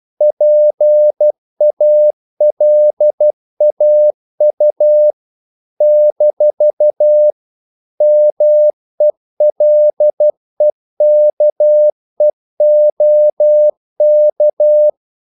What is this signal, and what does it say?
Morse code, heard as a keyed tone, about 12 wpm, 600 Hz. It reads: PALAU - MELEKEOK
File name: CW / Practice / NationalCapitals12